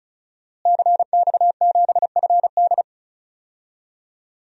Morse code, heard as a keyed tone, about 36 wpm, 700 Hz. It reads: CX7FD
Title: DXCC-100 Call Signs - 5 Characters - Repeated 2x - 35wpm